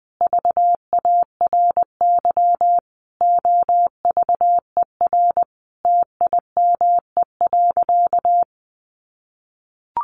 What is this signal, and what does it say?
Morse code, heard as a keyed tone, about 19 wpm, 700 Hz. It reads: VARY OVER TIME.
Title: Sentences from Top 1200 Words 20wpm